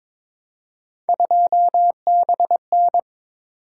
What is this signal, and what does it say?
Morse code, heard as a keyed tone, about 27 wpm, 700 Hz. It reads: 2BN
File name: Speed-Racing - Three Letter-Number Combinations - 22wpm